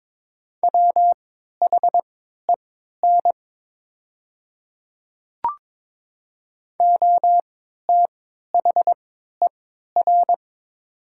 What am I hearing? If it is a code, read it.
Morse code, heard as a keyed tone, about 22 wpm, spaced as for 13 wpm, 700 Hz. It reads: WHEN OTHER
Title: 3x Character Spacing - Top 300 Words - 22wpm